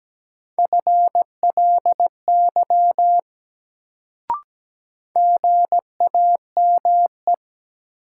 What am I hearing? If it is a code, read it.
Morse code, heard as a keyed tone, about 17 wpm, 700 Hz. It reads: FLY GAME
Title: Sentences from Top 500 Words - New Word Drill - 17wpm